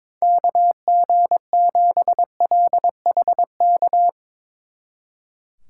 Morse code, heard as a keyed tone, about 24 wpm, 700 Hz. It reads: KG7LHK
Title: Speed Racing - USA Call Signs 22wpm